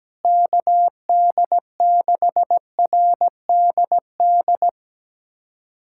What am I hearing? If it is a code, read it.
Morse code, heard as a keyed tone, about 17 wpm, 700 Hz. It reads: KD6RDD
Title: USA Callsign DE Callsign - 6 Characters - 17wpm